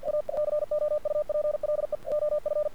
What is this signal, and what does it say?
Morse code, heard as a keyed tone, about 50 wpm, 600 Hz. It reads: A COUPLE OF
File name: archived / CRSnet